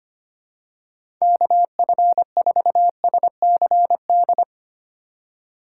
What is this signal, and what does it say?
Morse code, heard as a keyed tone, about 25 wpm, 700 Hz. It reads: KF4SCD
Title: USA Call Signs - 6 Characters - 3x Character Spacing - 25wpm